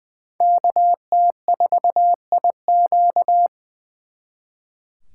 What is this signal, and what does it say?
Morse code, heard as a keyed tone, about 24 wpm, 700 Hz. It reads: KT4IQ
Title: Speed Racing - USA 2x2 Call Signs 20wpm